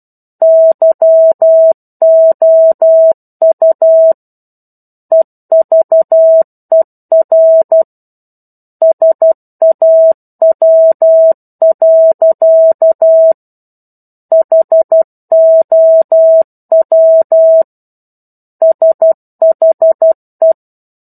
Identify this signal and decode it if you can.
Morse code, heard as a keyed tone, about 12 wpm, 650 Hz. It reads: YOU EVER SAW. HOW SHE
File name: 01-Chapter-01 12 WPM